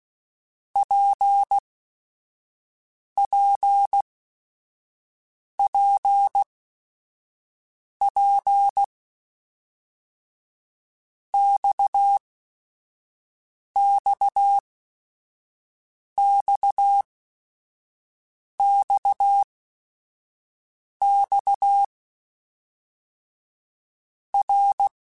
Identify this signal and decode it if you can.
Morse code, heard as a keyed tone, about 16 wpm, 800 Hz. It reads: P P P P X X X X X R
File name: Basic Sets / Basic Letter Groups Collective7 / RKPX@5